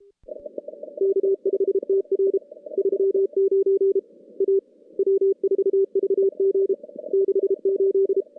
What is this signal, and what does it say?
Morse code, heard as a keyed tone, about 32 wpm, 390 Hz. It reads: K5TR 39 A W4VG 68
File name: contests / CW-SS-k5tr- -2300